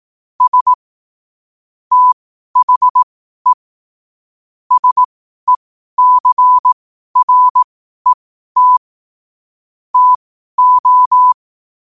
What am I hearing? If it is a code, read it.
Morse code, heard as a keyed tone, about 18 wpm, spaced as for 12 wpm, 1000 Hz. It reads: S THE SECRET TO
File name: qotdfiles / 12wpm